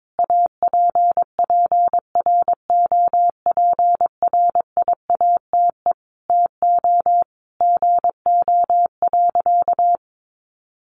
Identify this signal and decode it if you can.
Morse code, heard as a keyed tone, about 22 wpm, 700 Hz. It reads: APPROPRIATE TO GO.
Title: Sentences from Top 900 Words 22wpm